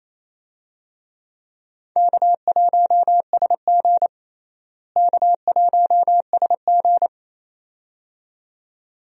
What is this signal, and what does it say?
Morse code, heard as a keyed tone, about 28 wpm, 700 Hz. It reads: K1SG K1SG
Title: USA Call Signs - 4 Characters - Repeated 2x - 28wpm